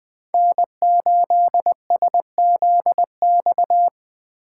Morse code, heard as a keyed tone, about 20 wpm, 700 Hz. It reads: N8SZX
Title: Speed Racing - USA 1x3 Call Signs 17wpm